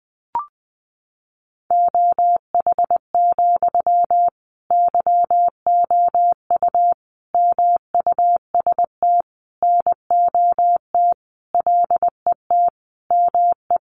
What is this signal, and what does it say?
Morse code, heard as a keyed tone, about 19 wpm, 700 Hz. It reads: OH, YOU MUST NOT LET ME
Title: Sentences from Top 1100 Words 20wpm